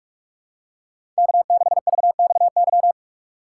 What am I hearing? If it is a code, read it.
Morse code, heard as a keyed tone, about 46 wpm, 700 Hz. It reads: K6VXY
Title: USA Call Signs - 5 Characters - 8x Character Spacing - 45wpm